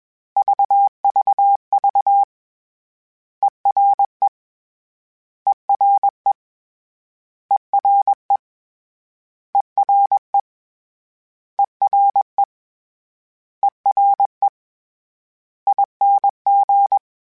Morse code, heard as a keyed tone, about 21 wpm, 800 Hz. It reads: VVV ERE ERE ERE ERE ERE ERE ING